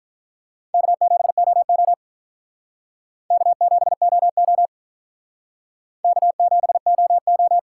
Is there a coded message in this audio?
Yes: Morse code from 0.7 to 7.6 s, heard as a keyed tone, about 47 wpm, 700 Hz. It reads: K7OO K7OO K7OO